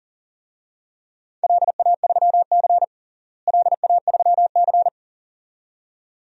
Morse code, heard as a keyed tone, about 40 wpm, 700 Hz. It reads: LA3C LA3C
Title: DXCC-100 Call Signs - 4 Characters - Repeated 2x - 40wpm